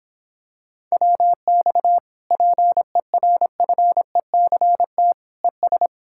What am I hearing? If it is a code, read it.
Morse code, heard as a keyed tone, about 26 wpm, 700 Hz. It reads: WX PERFECT ES